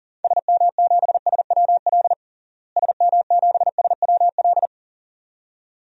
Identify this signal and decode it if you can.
Morse code, heard as a keyed tone, about 40 wpm, 700 Hz. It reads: SM7SWL SM7SWL
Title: DXCC-100 Call Signs - 6 Characters - Repeated 2x - 40wpm